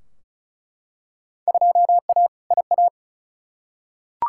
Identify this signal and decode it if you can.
Morse code, heard as a keyed tone, about 36 wpm, 700 Hz. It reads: 2A IA
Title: Field Day Exchanges Encore - 35wpm